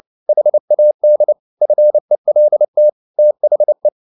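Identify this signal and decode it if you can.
Morse code, heard as a keyed tone, about 29 wpm, 600 Hz. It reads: HAD FELT THE
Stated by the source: Morse code at 29 wpm, sending Ken Follett's novel Never in English, from CW med Gnister.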